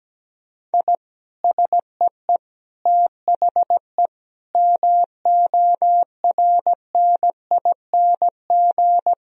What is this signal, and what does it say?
Morse code, heard as a keyed tone, about 17 wpm, 700 Hz. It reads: I SEE THE MORNING
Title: Sentences from Top 1000 Words - Review Part 2 - 17wpm